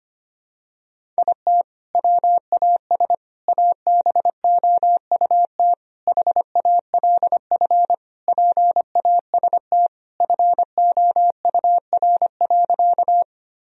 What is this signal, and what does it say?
Morse code, heard as a keyed tone, about 25 wpm, 700 Hz. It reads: IT WAS ABOUT HALF PAST FOUR.
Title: Sentences from Top 1000 Words - Review Part 3 - 25wpm